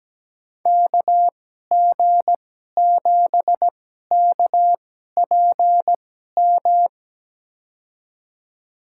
Morse code, heard as a keyed tone, about 17 wpm, spaced as for 12 wpm, 700 Hz. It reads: KG7KPM
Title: USA Call Signs - 6 Characters - 2x Character Spacing - 17wpm